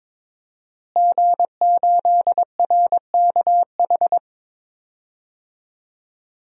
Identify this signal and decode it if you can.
Morse code, heard as a keyed tone, about 21 wpm, 700 Hz. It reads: G8RKH
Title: DXCC-100 Call Signs - 5 Characters - Repeated 2x - 22wpm